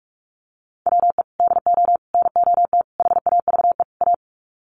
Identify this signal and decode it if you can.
Morse code, heard as a keyed tone, about 44 wpm, 700 Hz. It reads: WE DO NOT HAVE A
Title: Sets of 5 Words 45wpm